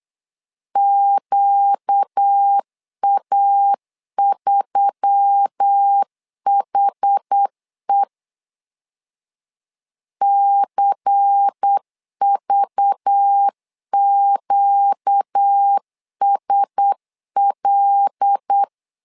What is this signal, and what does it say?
Morse code, heard as a keyed tone, about 8 wpm, 800 Hz. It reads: QA3HE CVQSL
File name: LEZIONI / 5 LBVC / 5 LBVC 8wpm 2935